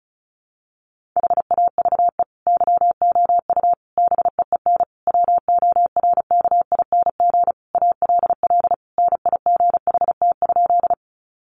Morse code, heard as a keyed tone, about 34 wpm, 700 Hz. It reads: HAVE YOU BEEN WORKING ALL NIGHT?
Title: Sentences from Top 700 Words 35wpm